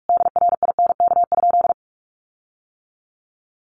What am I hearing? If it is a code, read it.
Morse code, heard as a keyed tone, about 44 wpm, 700 Hz. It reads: DRINK?
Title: Sentences from Top 1500 Words 45wpm